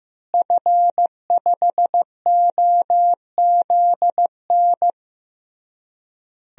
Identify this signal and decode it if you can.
Morse code, heard as a keyed tone, about 15 wpm, 700 Hz. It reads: F5OZN
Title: DXCC-100 Call Signs - 5 Characters - Repeated 3x - 15wpm